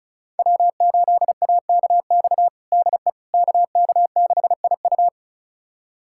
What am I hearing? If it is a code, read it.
Morse code, heard as a keyed tone, about 36 wpm, 690 Hz. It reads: W8AKX DE KK6IU
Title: USA Callsign DE Callsign - 5 Characters - 35wpm